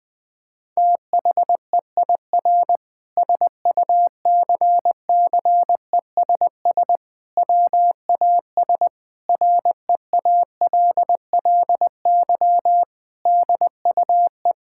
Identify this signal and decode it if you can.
Morse code, heard as a keyed tone, about 20 wpm, 700 Hz. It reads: THEIR SUCCESS WAS REALLY DUE
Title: Sentences from Top 1000 Words - Review Part 6 - 20wpm